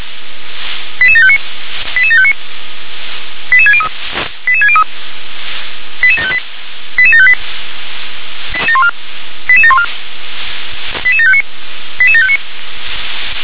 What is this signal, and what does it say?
Five-tone selective call (ZVEI), heard as quick runs of five tones, tones of about 70 ms. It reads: 88759 88759 88603 88603 88758 88758 88714 88759 88759